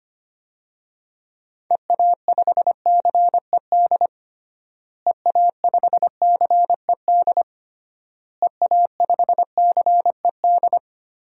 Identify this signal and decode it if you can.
Morse code, heard as a keyed tone, about 25 wpm, 700 Hz. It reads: EA5CED EA5CED EA5CED
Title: DXCC-100 Call Signs - 6 Characters - Repeated 3x - 25wpm